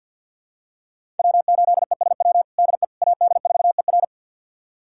Morse code, heard as a keyed tone, about 52 wpm, 700 Hz. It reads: W8ESW DE AD4ER
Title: USA Callsign DE Callsign - 5 Characters - 50wpm